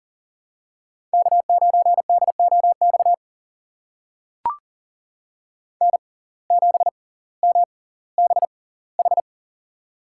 Morse code, heard as a keyed tone, about 40 wpm, spaced as for 20 wpm, 700 Hz. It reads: K9DOX N7MBH
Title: USA Call Signs - 5 Characters - 6x Character Spacing - 40wpm